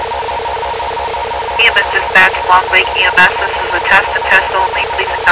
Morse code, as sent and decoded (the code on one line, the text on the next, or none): none